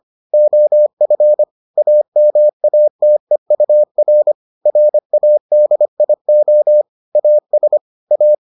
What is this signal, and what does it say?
Morse code, heard as a keyed tone, about 25 wpm, 600 Hz. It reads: OF AMATEUR RADIO AS A